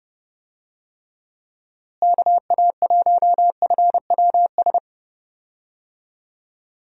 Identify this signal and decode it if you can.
Morse code, heard as a keyed tone, about 29 wpm, 700 Hz. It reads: KA1FWS
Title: USA Call Signs - 6 Characters - Repeated 2x - 30wpm